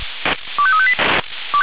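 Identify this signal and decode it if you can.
Five-tone selective call (ZVEI), heard as quick runs of five tones, tones of about 70 ms. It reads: 25358
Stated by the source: ZVEI five-tone call 25344 received by SDR, channel 496.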